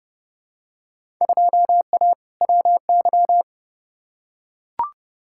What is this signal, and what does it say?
Morse code, heard as a keyed tone, about 30 wpm, 700 Hz. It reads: 2A WY
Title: Field Day Exchanges - 30wpm